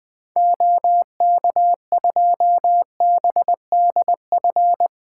Morse code, heard as a keyed tone, about 20 wpm, 700 Hz. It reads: OK2BDF
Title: 2023 SOTA Activator Call Signs - 20wpm